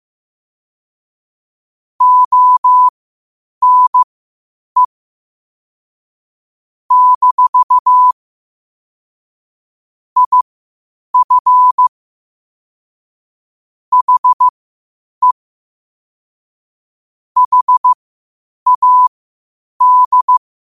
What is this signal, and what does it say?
Morse code, heard as a keyed tone, about 15 wpm, spaced as for 8 wpm, 1000 Hz. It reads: ONE - IF HE HAD
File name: qotdfiles / 8wpm